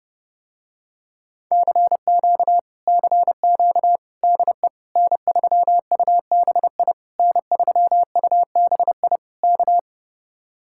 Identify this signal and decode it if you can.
Morse code, heard as a keyed tone, about 30 wpm, 700 Hz. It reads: CQ CQ DE N3UBI N3UBI K